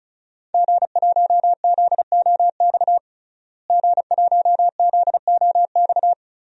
Morse code, heard as a keyed tone, about 36 wpm, 700 Hz. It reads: G1ZOX G1ZOX
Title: DXCC-100 Call Signs - 5 Characters - Repeated 3x - 35wpm